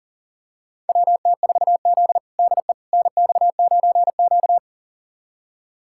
Morse code, heard as a keyed tone, about 40 wpm, 700 Hz. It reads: WT4Z DE NX9Q
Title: USA Callsign DE Callsign - 4 Characters - 40wpm